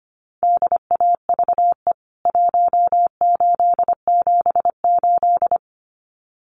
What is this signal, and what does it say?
Morse code, heard as a keyed tone, about 25 wpm, 700 Hz. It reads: DAVE 1878
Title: CWT Exchange 25wpm